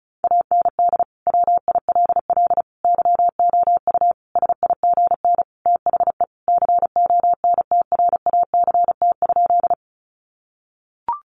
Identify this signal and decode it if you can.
Morse code, heard as a keyed tone, about 34 wpm, 700 Hz. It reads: AND WILL YOU SIGN THE CONTRACT?
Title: Sentences from Top 1400 Words 35wpm